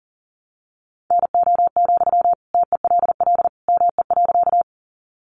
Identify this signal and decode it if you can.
Morse code, heard as a keyed tone, about 40 wpm, 700 Hz. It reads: NO, TELL ME.
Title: Sentences from Top 200 Words 40wpm